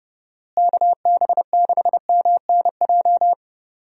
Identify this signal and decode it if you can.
Morse code, heard as a keyed tone, about 32 wpm, 700 Hz. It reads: KB6MNJ
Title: USA Call Signs - 6 Characters - Repeated 2x - 30wpm